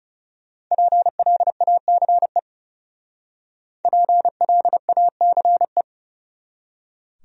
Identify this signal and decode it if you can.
Morse code, heard as a keyed tone, about 39 wpm, 700 Hz. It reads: PLACE PLACE